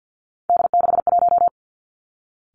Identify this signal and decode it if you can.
Morse code, heard as a keyed tone, about 50 wpm, 700 Hz. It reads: D61
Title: Three Challenging Letter-Number Combinations 50wpm